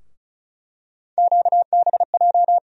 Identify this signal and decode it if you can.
Morse code, heard as a keyed tone, about 36 wpm, 700 Hz. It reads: QBJ
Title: USA Call Sign Postfixes - 3 Letters 35wpm